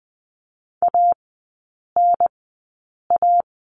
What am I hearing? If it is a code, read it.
Morse code, heard as a keyed tone, about 20 wpm, spaced as for 8 wpm, 700 Hz. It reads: ANA
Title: 041 Lesson 8e - Number 1 - Three Characters 20wpm